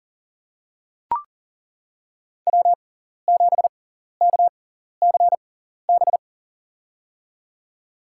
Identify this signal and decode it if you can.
Morse code, heard as a keyed tone, about 40 wpm, spaced as for 14 wpm, 700 Hz. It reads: W7KCB
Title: USA Call Signs - 5 Characters - 6x Character Spacing - 40wpm